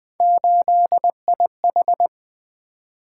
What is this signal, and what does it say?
Morse code, heard as a keyed tone, about 20 wpm, 700 Hz. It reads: 8IH
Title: Three Letter-Number Combinations - 20wpm